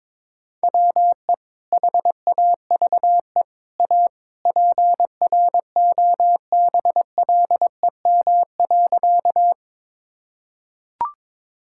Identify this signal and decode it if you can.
Morse code, heard as a keyed tone, about 22 wpm, 700 Hz. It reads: WE HAVE A PROBLEM.
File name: practice / 22wpm / Sentences with 3-Word Phrases - 22wpm